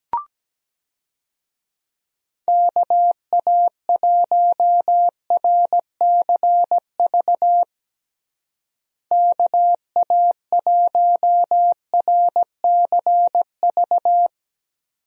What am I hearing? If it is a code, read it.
Morse code, heard as a keyed tone, about 17 wpm, 700 Hz. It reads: KA1RCV KA1RCV
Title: USA Call Signs - 6 Characters - Repeated 2x - 17wpm